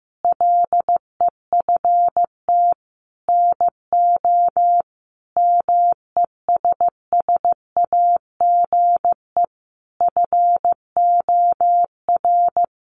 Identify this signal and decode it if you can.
Morse code, heard as a keyed tone, about 15 wpm, 700 Hz. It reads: LEFT NO MESSAGE FOR
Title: Sentences from Top 1100 Words 15wpm